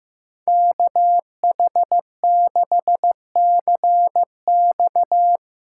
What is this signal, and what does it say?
Morse code, heard as a keyed tone, about 17 wpm, 700 Hz. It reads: KH6CX
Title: Speed Racing - USA 2x2 Call Signs 15wpm